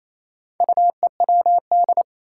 Morse code, heard as a keyed tone, about 28 wpm, 700 Hz. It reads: UEWD
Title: Four Challenging Letter-Number Combinations 28wpm